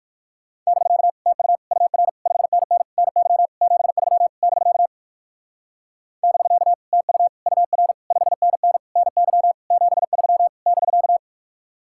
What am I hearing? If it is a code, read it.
Morse code, heard as a keyed tone, about 58 wpm, 700 Hz. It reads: <BK> TU UR 5NN NY 73 <BK> <BK> TU UR 5NN NY 73 <BK>